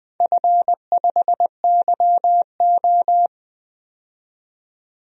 Morse code, heard as a keyed tone, about 20 wpm, 700 Hz. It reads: F5YO
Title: DXCC-100 Call Signs - 4 Characters - 20wpm